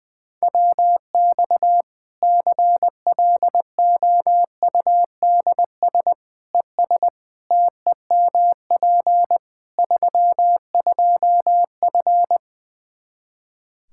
Morse code, heard as a keyed tone, about 20 wpm, 700 Hz. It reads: WX CLOUDS ES TEMP 32F
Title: Practice QSO - Exchange 2 - Info - 20wpm